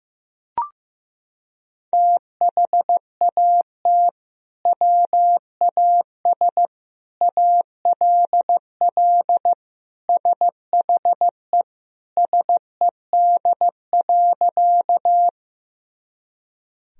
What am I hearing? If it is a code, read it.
Morse code, heard as a keyed tone, about 15 wpm, 700 Hz. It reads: THAT WAS ALL SHE SED.
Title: Sentences from Top 100 Words and Abbreviations 15wpm